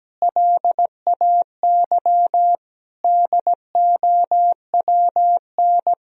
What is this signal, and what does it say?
Morse code, heard as a keyed tone, about 17 wpm, 700 Hz. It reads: LAY DOWN
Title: Sentences from Top 1000 Words - Review Part 2 - 17wpm